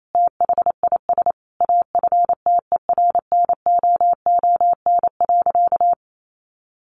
Morse code, heard as a keyed tone, about 28 wpm, 700 Hz. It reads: THIS AFTERNOON.
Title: Sentences from Top 1500 Words 28wpm